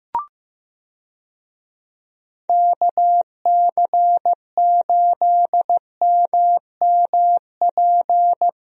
Morse code, heard as a keyed tone, about 16 wpm, 700 Hz. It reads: KC8MMP
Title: USA Call Signs - 6 Characters - Repeated 3x - 15wpm